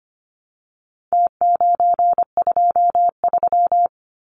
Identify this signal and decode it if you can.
Morse code, heard as a keyed tone, about 25 wpm, 700 Hz. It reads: T923
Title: Numbers - Contest Style - 4-Digit 25wpm